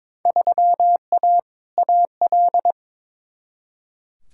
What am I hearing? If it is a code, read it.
Morse code, heard as a keyed tone, about 22 wpm, 700 Hz. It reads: 3A AL
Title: Field Day Exchanges - 22wpm